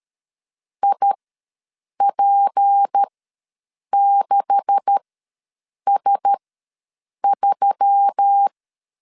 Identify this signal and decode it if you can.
Morse code, heard as a keyed tone, about 13 wpm, spaced as for 7 wpm, 800 Hz. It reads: IP6S3
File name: LEZIONI / 10 SIGNS / 10A SLASH ws 12wpm 3443